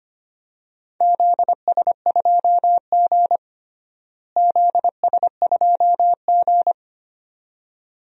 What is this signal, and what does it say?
Morse code, heard as a keyed tone, about 25 wpm, 700 Hz. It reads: ZS2G ZS2G
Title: DXCC-100 Call Signs - 4 Characters - Repeated 2x - 25wpm